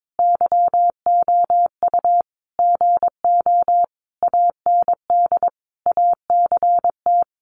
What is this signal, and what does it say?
Morse code, heard as a keyed tone, about 22 wpm, 700 Hz. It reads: YOU GO AND ACT
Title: Sentences from Top 1200 Words 22wpm